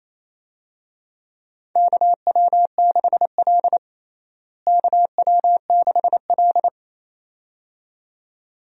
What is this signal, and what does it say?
Morse code, heard as a keyed tone, about 28 wpm, 700 Hz. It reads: KW6L KW6L
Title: USA Call Signs - 4 Characters - Repeated 2x - 28wpm